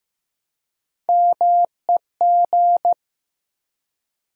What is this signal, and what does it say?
Morse code, heard as a keyed tone, about 15 wpm, 700 Hz. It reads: MEG
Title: USA Common Names - Max 5 Letters - 15wpm